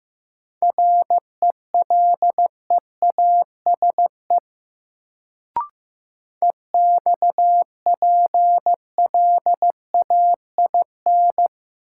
Morse code, heard as a keyed tone, about 15 wpm, 700 Hz. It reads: RELEASE EXPLAIN
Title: Sentences from Top 1100 Words - New Word Drill - 15wpm